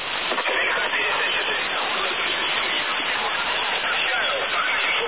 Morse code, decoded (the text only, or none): none